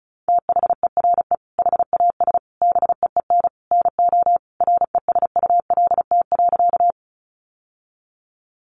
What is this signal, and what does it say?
Morse code, heard as a keyed tone, about 34 wpm, 700 Hz. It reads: THERE HAS BEEN NO RESULT.
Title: Sentences from Top 700 Words 35wpm